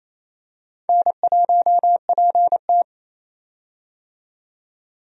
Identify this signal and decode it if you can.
Morse code, heard as a keyed tone, about 27 wpm, 700 Hz. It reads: N1PT
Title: USA Call Signs - 4 Characters - 28wpm